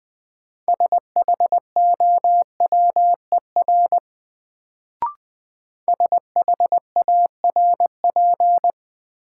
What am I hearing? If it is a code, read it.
Morse code, heard as a keyed tone, about 20 wpm, 700 Hz. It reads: SHOWER SHARP
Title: Sentences from Top 1600 Words - New Word Drill Alt - 20wpm